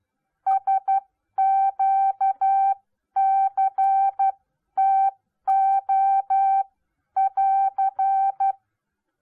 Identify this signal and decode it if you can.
Morse code, heard as a keyed tone, about 12 wpm, spaced as for 10 wpm, 780 Hz. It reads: SQCTO<AR>